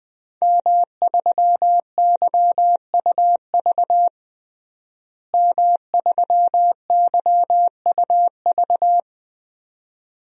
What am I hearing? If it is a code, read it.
Morse code, heard as a keyed tone, about 20 wpm, 700 Hz. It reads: M3YUV M3YUV
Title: DXCC-100 Call Signs - 5 Characters - Repeated 2x - 20wpm